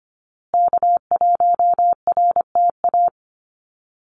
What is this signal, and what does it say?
Morse code, heard as a keyed tone, about 25 wpm, 700 Hz. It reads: K1RTA
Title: USA Call Signs Repeated 2x 25wpm